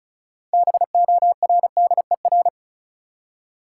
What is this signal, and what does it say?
Morse code, heard as a keyed tone, about 36 wpm, 700 Hz. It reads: BORDER
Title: Six Letter Words - 35wpm